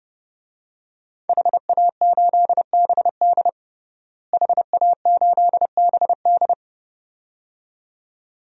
Morse code, heard as a keyed tone, about 30 wpm, 700 Hz. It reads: HA8BD HA8BD
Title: DXCC-100 Call Signs - 5 Characters - Repeated 2x - 30wpm